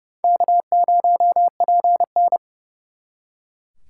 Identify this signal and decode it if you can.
Morse code, heard as a keyed tone, about 37 wpm, 700 Hz. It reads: K0PN
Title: Speed Racing - USA 1x2 Call Signs 30wpm